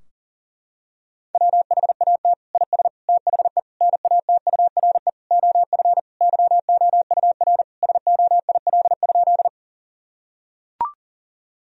Morse code, heard as a keyed tone, about 40 wpm, 700 Hz. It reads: WHAT IS THE NATURE OF YOUR SOIL?
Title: Sentences from Top 1000 Words - Review Part 4 - 40wpm